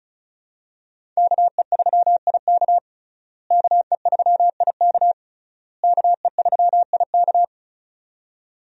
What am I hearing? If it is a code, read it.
Morse code, heard as a keyed tone, about 36 wpm, 700 Hz. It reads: KE3IK KE3IK KE3IK
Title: USA Call Signs - 5 Characters - Repeated 3x - 35wpm